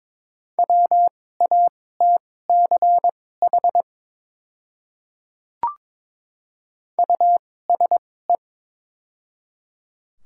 Morse code, heard as a keyed tone, about 22 wpm, spaced as for 16 wpm, 700 Hz. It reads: WATCH USE
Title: 2x Character Spacing - Top 500 Words - 22wpm